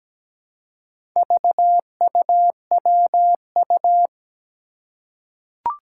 Morse code, heard as a keyed tone, about 17 wpm, 700 Hz. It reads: VUWU
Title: Four Challenging Letter-Number Combinations 17wpm